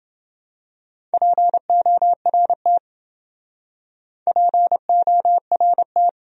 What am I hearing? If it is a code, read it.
Morse code, heard as a keyed tone, about 28 wpm, 700 Hz. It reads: PORT PORT